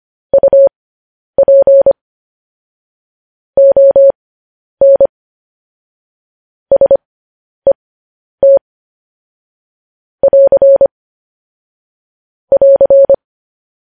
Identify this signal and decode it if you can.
Morse code, heard as a keyed tone, about 25 wpm, spaced as for 10 wpm, 550 Hz. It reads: UP ON SET <AR> <AR>
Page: News Headlines in Morse Code at 10 WPM